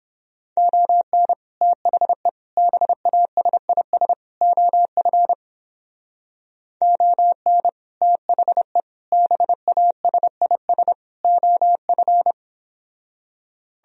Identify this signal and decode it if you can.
Morse code, heard as a keyed tone, about 31 wpm, 700 Hz. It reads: ON THE BASIS OF ON THE BASIS OF